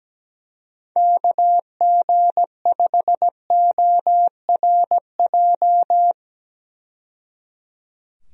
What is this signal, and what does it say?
Morse code, heard as a keyed tone, about 17 wpm, 700 Hz. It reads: KG5ORJ
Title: USA Call Signs - 6 Characters - 17wpm